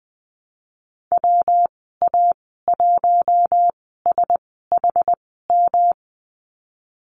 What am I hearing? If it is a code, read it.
Morse code, heard as a keyed tone, about 20 wpm, spaced as for 15 wpm, 700 Hz. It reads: WA1SHM
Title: 147 Lesson 24g - Letter y - Call Signs 20wpm